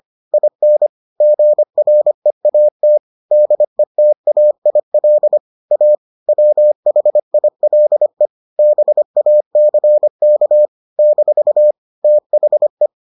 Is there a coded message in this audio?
Yes: Morse code from 0.3 to 12.9 s, heard as a keyed tone, about 25 wpm, 600 Hz. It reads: IN GREAT DETAIL A WHILE BACK - THE